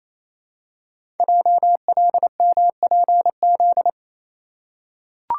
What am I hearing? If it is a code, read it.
Morse code, heard as a keyed tone, about 29 wpm, 700 Hz. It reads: JLMPZ
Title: Code Groups - 5 Letters - 28wpm